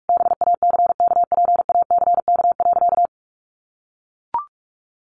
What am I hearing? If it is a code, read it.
Morse code, heard as a keyed tone, about 44 wpm, 700 Hz. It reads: BACKPACK.
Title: Additional Easy Phrases 45wpm